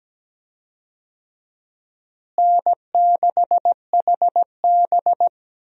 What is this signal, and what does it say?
Morse code, heard as a keyed tone, about 17 wpm, 700 Hz. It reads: N6HB
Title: USA Call Signs - 4 Characters - Repeated 2x - 17wpm